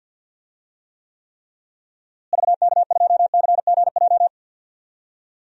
Morse code, heard as a keyed tone, about 52 wpm, 700 Hz. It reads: VK2CGJ
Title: DXCC-100 Call Signs - 6 Characters - Repeated 3x - 50wpm